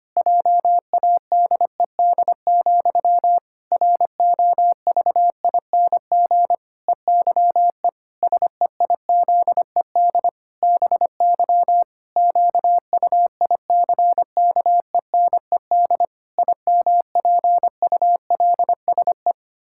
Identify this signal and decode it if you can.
Morse code, heard as a keyed tone, about 25 wpm, 700 Hz. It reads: JADED, ROVING EYE SEIZED BY QUICKENED IMPULSE
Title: Pangram Challenge v2 - 25wpm